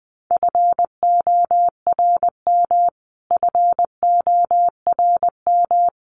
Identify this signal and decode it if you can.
Morse code, heard as a keyed tone, about 20 wpm, 700 Hz. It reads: FORM FORM
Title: Top 300 Words Repeated 20wpm